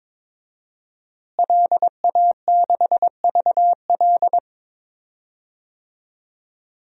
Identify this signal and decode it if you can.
Morse code, heard as a keyed tone, about 22 wpm, 700 Hz. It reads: LA6VL
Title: DXCC-100 Call Signs - 5 Characters - Repeated 3x - 22wpm